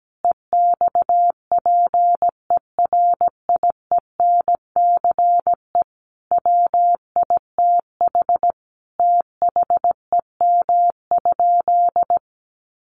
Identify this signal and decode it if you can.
Morse code, heard as a keyed tone, about 17 wpm, 700 Hz. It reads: EXPERIENCE WITH THEM?
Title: Sentences from Top 600 Words 17wpm